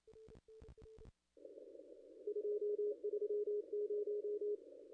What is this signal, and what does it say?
Morse code, heard as a keyed tone, about 27 wpm, 410 Hz. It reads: 230
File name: contests / ss-cw-k5tr-op-n5rz -0200